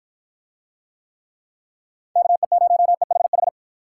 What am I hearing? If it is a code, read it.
Morse code, heard as a keyed tone, about 51 wpm, 700 Hz. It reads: KE0EHH